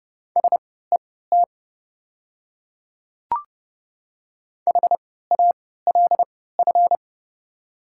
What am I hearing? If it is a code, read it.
Morse code, heard as a keyed tone, about 30 wpm, spaced as for 17 wpm, 700 Hz. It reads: SET HALF
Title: 3x Character Spacing - Top 500 Words - 30wpm